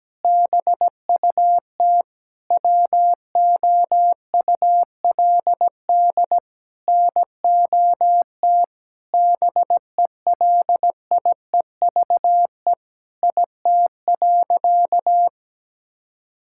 Morse code, heard as a keyed tone, about 17 wpm, 700 Hz. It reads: BUT WOULD NOT BELIEVE IT.
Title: Sentences from Top 1000 Words - Review Part 6 - 17wpm